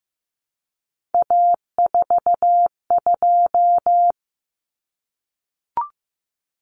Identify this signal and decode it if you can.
Morse code, heard as a keyed tone, about 15 wpm, 700 Hz. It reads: A42
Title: Common Cut Numbers - 3-Digit 15wpm